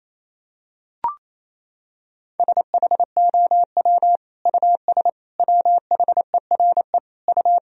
Morse code, heard as a keyed tone, about 28 wpm, 700 Hz. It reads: SHOW US WHERE U
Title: Sentences from Top 200 Words and Abbreviations 28wpm